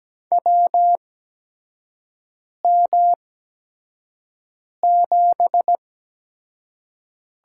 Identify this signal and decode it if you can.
Morse code, heard as a keyed tone, about 17 wpm, spaced as for 5 wpm, 700 Hz. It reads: WM7
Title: USA Call Signs - 8x Character Spacing - 17wpm